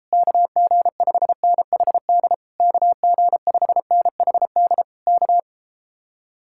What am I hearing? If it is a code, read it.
Morse code, heard as a keyed tone, about 38 wpm, 700 Hz. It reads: KG5NHD KG5NHD K